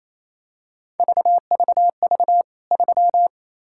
Morse code, heard as a keyed tone, about 27 wpm, 700 Hz. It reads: VVV 3
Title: K1USN SST QSO Practice 28wpm